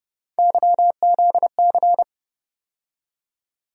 Morse code, heard as a keyed tone, about 30 wpm, 700 Hz. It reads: YZC
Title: USA Call Sign Postfixes - 3 Letters 30wpm